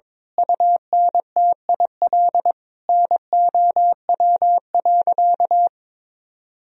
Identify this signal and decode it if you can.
Morse code, heard as a keyed tone, about 22 wpm, 700 Hz. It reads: UNTIL NOW.
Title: Sentences from Top 1000 Words - Review Part 3 - 22wpm